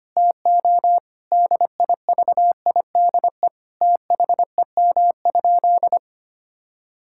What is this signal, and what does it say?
Morse code, heard as a keyed tone, about 25 wpm, 700 Hz. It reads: TO DIVIDE THEM?
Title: Sentences from Top 1600 Words 25wpm